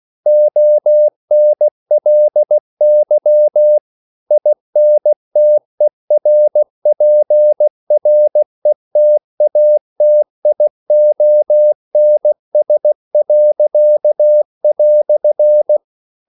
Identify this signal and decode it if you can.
Morse code, heard as a keyed tone, about 16 wpm, 600 Hz. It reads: ONLY INTERPRETATIONS."
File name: CW / Practice / EnglishQuotations0116